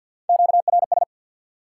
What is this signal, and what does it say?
Morse code, heard as a keyed tone, about 51 wpm, 700 Hz. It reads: QRS
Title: Top 90 QSO Elements 50wpm